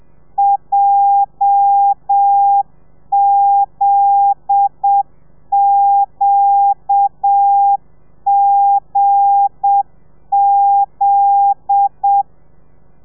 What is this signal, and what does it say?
Morse code, heard as a keyed tone, about 7 wpm, 800 Hz. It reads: JZQGZ